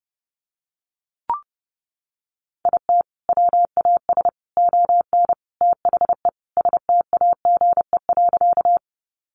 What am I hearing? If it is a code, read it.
Morse code, heard as a keyed tone, about 28 wpm, 700 Hz. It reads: IT WAS ON THE STAGE.
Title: Sentences from Top 1200 Words 30wpm